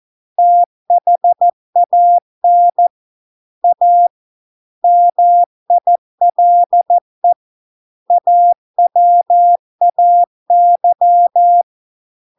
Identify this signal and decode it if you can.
Morse code, heard as a keyed tone, about 14 wpm, 700 Hz. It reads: THAN A MILE AWAY